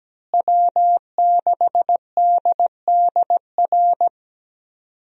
Morse code, heard as a keyed tone, about 17 wpm, 700 Hz. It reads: W6DDR
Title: USA Call Signs - 5 Characters - Repeated 3x - 17wpm